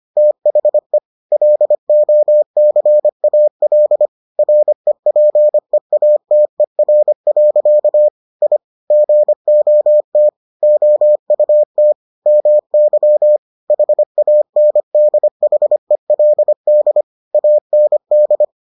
Morse code, heard as a keyed tone, about 25 wpm, 600 Hz. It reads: THE LOCAL REPEATER. I GOT OUT MY HANDHELD AND